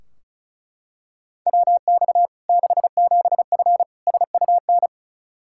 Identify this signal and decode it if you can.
Morse code, heard as a keyed tone, about 36 wpm, 700 Hz. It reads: WX 67F SUN
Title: K1USN SST QSO Practice 35wpm